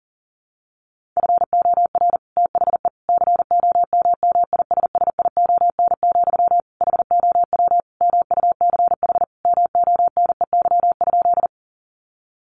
Morse code, heard as a keyed tone, about 40 wpm, 700 Hz. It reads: FOR THE COMMISSION, HOW MUCH MONEY?
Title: Sentences from Top 1400 Words 40wpm